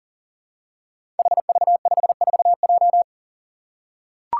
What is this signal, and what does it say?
Morse code, heard as a keyed tone, about 40 wpm, 700 Hz. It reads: HV54J
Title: Five Challenging Letter-Number Combinations 40wpm